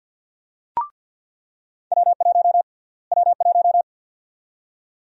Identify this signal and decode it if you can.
Morse code, heard as a keyed tone, about 49 wpm, 700 Hz. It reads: W1 W1
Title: USA Call Sign Prefixes - Repeated 2x - 50wpm